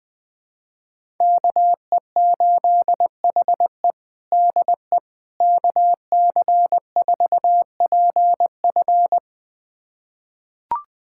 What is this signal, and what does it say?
Morse code, heard as a keyed tone, about 20 wpm, 700 Hz. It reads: KE8HE DE KC4PF
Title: USA Callsign DE Callsign - 5 Characters - 20wpm